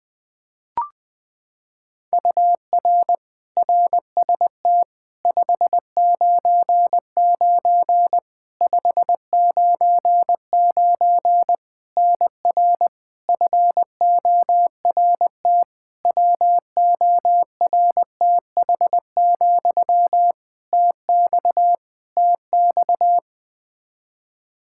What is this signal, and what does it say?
Morse code, heard as a keyed tone, about 21 wpm, 700 Hz. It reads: UR RST 599 599 NR FORT WORTH, TX TX
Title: Practice QSO - Exchange 1 - Beginning - 20wpm